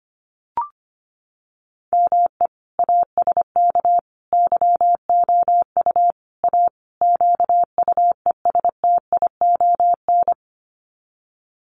Morse code, heard as a keyed tone, about 25 wpm, 700 Hz. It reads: ME ASK YOU A QUESTION
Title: Sets of 5 Words Encore2 25wpm